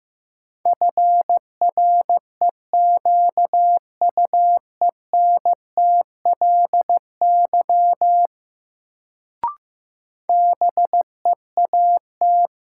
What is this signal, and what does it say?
Morse code, heard as a keyed tone, about 15 wpm, 700 Hz. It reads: FREQUENTLY BEAT
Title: Sentences from Top 1300 Words - New Word Drill Alt - 15wpm